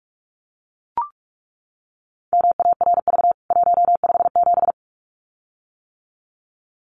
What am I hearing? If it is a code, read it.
Morse code, heard as a keyed tone, about 44 wpm, 700 Hz. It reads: MARV 157
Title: CWT Exchange 45wpm